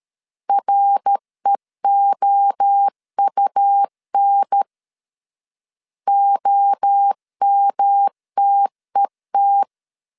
Morse code, heard as a keyed tone, about 13 wpm, 800 Hz. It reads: REOUN OMTET
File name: CW / DET / 1 3 1500C S B 12wpm 2828